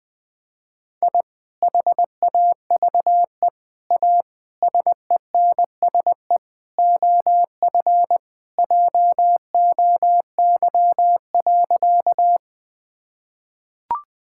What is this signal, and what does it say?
Morse code, heard as a keyed tone, about 20 wpm, 700 Hz. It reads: I HAVE A SENSE OF JOY.
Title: Sentences with 3-Word Phrases - Spoken Beginning - 20wpm